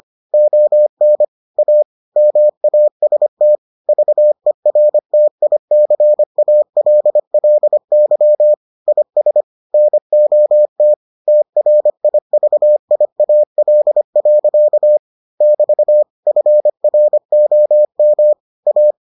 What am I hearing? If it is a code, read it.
Morse code, heard as a keyed tone, about 25 wpm, 600 Hz. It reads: ON A MAST VERTICALLY IS NOT TRIVIAL. <BT> FROM A